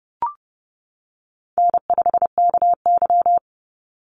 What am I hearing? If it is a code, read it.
Morse code, heard as a keyed tone, about 30 wpm, 700 Hz. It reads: N5KY
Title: Call Signs - 30wpm